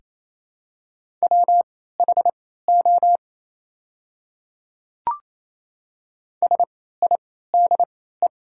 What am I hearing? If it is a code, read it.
Morse code, heard as a keyed tone, about 28 wpm, spaced as for 16 wpm, 700 Hz. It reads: WHO SIDE
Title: 3x Character Spacing - Top 200 Words - 28wpm